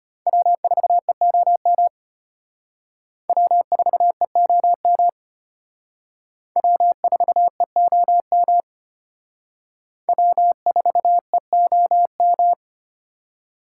Morse code, heard as a keyed tone, about 32 wpm, 700 Hz. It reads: W4EOM W4EOM W4EOM W4EOM